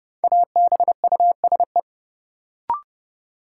Morse code, heard as a keyed tone, about 30 wpm, 700 Hz. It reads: ABUSE
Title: Sentences from Top 1300 Words - New Word Drill Alt - 30wpm